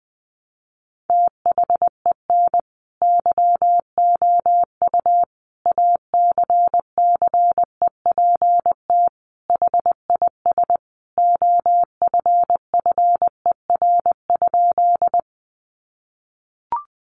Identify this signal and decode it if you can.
Morse code, heard as a keyed tone, about 19 wpm, 700 Hz. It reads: THEN YOU ACCEPT HIS OFFER?
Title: Sentences from Top 1300 Words 20wpm